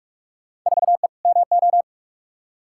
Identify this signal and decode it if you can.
Morse code, heard as a keyed tone, about 46 wpm, 700 Hz. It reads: 4E MO
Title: Field Day Exchanges Encore - 45wpm